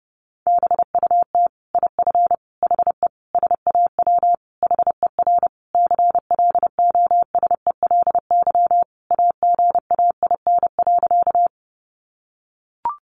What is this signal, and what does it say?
Morse code, heard as a keyed tone, about 28 wpm, 700 Hz. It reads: BUT IF HE SAW HER CLOSELY AGAIN.
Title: Sentences from Top 1400 Words 30wpm